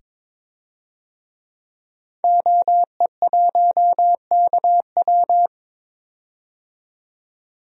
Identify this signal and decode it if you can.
Morse code, heard as a keyed tone, about 22 wpm, 700 Hz. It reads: OE1KW
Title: DXCC-100 Call Signs - 5 Characters - 22wpm